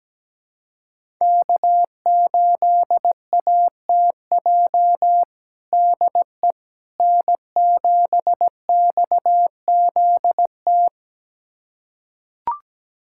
Morse code, heard as a keyed tone, about 17 wpm, 700 Hz. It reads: K8ATJ DE N7XZT
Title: USA Callsign DE Callsign - 5 Characters - 17wpm